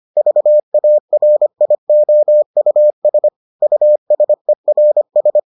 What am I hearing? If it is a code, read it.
Morse code, heard as a keyed tone, about 25 wpm, 600 Hz. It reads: VARIOUS USERS